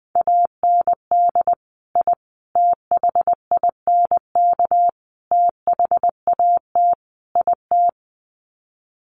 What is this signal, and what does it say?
Morse code, heard as a keyed tone, about 20 wpm, 700 Hz. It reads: AND I THINK THAT IT
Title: Sets of 5 Words Encore2 20wpm